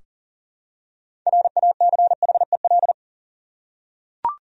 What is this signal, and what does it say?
Morse code, heard as a keyed tone, about 40 wpm, 700 Hz. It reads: RACHEL
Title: USA Common Names - 40wpm